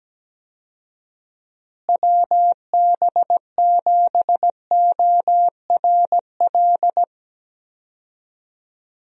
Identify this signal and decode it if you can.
Morse code, heard as a keyed tone, about 17 wpm, 700 Hz. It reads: WB7ORL
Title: USA Call Signs - 6 Characters - Repeated 3x - 17wpm